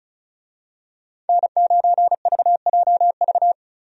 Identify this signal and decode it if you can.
Morse code, heard as a keyed tone, about 36 wpm, 700 Hz. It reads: N9VJV
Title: USA Call Signs - 5 Characters - 35wpm